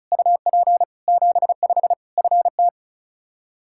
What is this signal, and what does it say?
Morse code, heard as a keyed tone, about 36 wpm, 700 Hz. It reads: UP 75 FT
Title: Practice QSO - Exchange 2 - Info - 35wpm